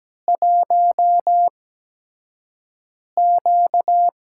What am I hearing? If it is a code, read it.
Morse code, heard as a keyed tone, about 17 wpm, spaced as for 5 wpm, 700 Hz. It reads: 1Q
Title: USA Call Signs - 8x Character Spacing - 17wpm